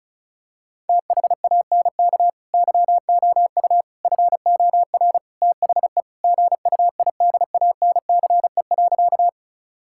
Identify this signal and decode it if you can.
Morse code, heard as a keyed tone, about 36 wpm, 700 Hz. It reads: THANK YOU FOR THE GUIDANCE.
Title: Sentences from Top 1600 Words 35wpm